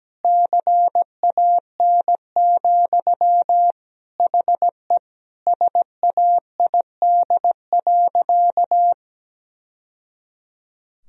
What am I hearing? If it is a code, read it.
Morse code, heard as a keyed tone, about 17 wpm, 700 Hz. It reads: CAN, HE SAID.
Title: Sentences from Top 1000 Words - Review Part 1 - 17wpm